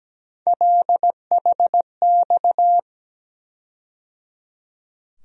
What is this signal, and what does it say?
Morse code, heard as a keyed tone, about 17 wpm, 700 Hz. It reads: LHX
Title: USA Call Sign Postfixes - 3 Letters 17wpm